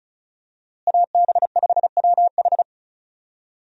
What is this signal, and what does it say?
Morse code, heard as a keyed tone, about 35 wpm, 700 Hz. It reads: AB5WH
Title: USA Call Signs - 5 Characters - Repeated 2x - 35wpm